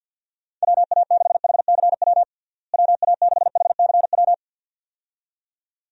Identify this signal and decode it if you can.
Morse code, heard as a keyed tone, about 53 wpm, 700 Hz. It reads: WA6HCW WA6HCW
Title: USA Call Signs - 6 Characters - Repeated 2x - 50wpm